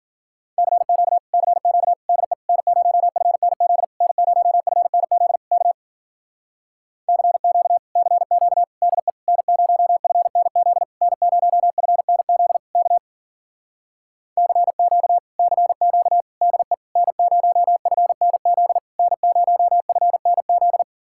Morse code, heard as a keyed tone, about 52 wpm, 700 Hz. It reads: CQ CQ DE N0FNZ N0FNZ K CQ CQ DE N0FNZ N0FNZ K CQ CQ DE N0FNZ N0FNZ